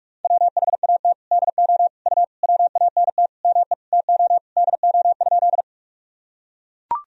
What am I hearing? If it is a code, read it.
Morse code, heard as a keyed tone, about 43 wpm, 700 Hz. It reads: WHAT DO U WANT ME TO DO?
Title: Sentences from Top 200 Words and Abbreviations 45wpm